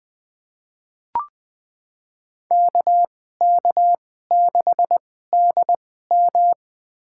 Morse code, heard as a keyed tone, about 21 wpm, spaced as for 15 wpm, 700 Hz. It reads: KK6DM
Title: USA Call Signs - 5 Characters - 2x Character Spacing - 20wpm